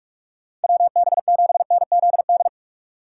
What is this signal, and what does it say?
Morse code, heard as a keyed tone, about 42 wpm, 690 Hz. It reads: WB7NZD
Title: USA Call Signs - 6 Characters - 2x Character Spacing - 45wpm